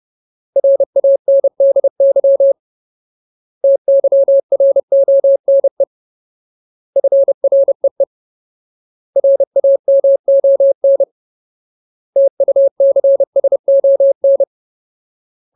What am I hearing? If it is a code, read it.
Morse code, heard as a keyed tone, about 30 wpm, 550 Hz. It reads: RANDY TYRONE FREE RAMON TUCSON
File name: Ultimate Word List / 30WPM Ultimate